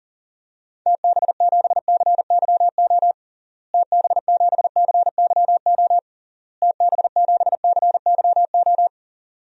Morse code, heard as a keyed tone, about 37 wpm, 700 Hz. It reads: TB7CYO TB7CYO TB7CYO
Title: DXCC-100 Call Signs - 6 Characters - Repeated 3x - 40wpm